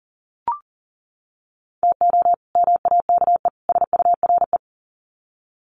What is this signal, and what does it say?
Morse code, heard as a keyed tone, about 40 wpm, 700 Hz. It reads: TO MAKE SURE
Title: Sets of 3 Words 40wpm